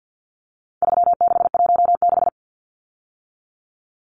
Morse code, heard as a keyed tone, about 50 wpm, 700 Hz. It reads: 3616
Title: Numbers - 4-Digit 50wpm